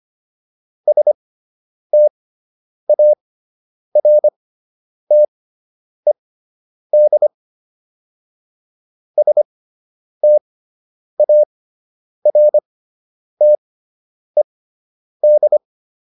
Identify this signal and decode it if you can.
Morse code, heard as a keyed tone, about 25 wpm, spaced as for 9 wpm, 600 Hz. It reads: STARTED STARTED